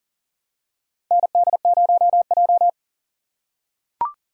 Morse code, heard as a keyed tone, about 40 wpm, 700 Hz. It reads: ND0J
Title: USA Call Signs - 3x Character Spacing - 40wpm